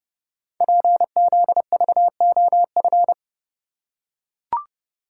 Morse code, heard as a keyed tone, about 30 wpm, 700 Hz. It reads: PZVOF
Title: Code Groups - 5 Letters - 30wpm